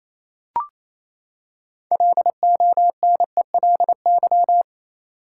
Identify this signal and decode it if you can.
Morse code, heard as a keyed tone, about 28 wpm, 700 Hz. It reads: LONELY
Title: Six Letter Words - 28wpm